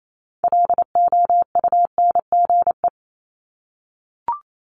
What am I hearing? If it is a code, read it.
Morse code, heard as a keyed tone, about 28 wpm, 700 Hz. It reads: LOUNGE
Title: Multi-letter Phonemes 28wpm